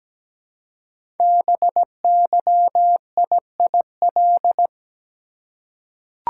Morse code, heard as a keyed tone, about 17 wpm, 700 Hz. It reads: BYIIL
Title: Code Groups - 5 Letters - 17wpm